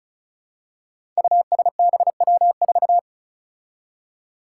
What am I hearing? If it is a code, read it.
Morse code, heard as a keyed tone, about 36 wpm, 700 Hz. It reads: USBW4
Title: Five Challenging Letter-Number Combinations 35wpm